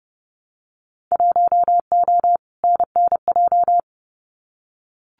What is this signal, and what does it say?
Morse code, heard as a keyed tone, about 30 wpm, 700 Hz. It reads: 1O NNJ
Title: Winter Field Day Exchanges 30wpm